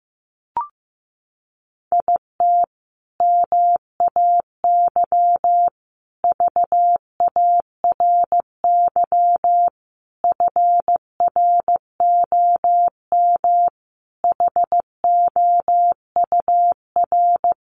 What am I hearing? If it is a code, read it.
Morse code, heard as a keyed tone, about 15 wpm, 700 Hz. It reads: IT MAY VARY FROM HOUR
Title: Sentences from Top 1200 Words 15wpm